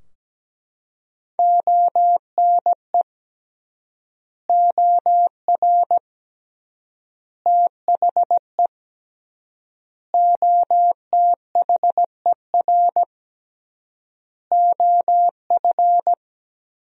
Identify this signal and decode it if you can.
Morse code, heard as a keyed tone, about 17 wpm, 700 Hz. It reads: ONE OR THE OTHER OF
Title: 3x Word-Spacing - Sentences from Top 400 Words - 17wpm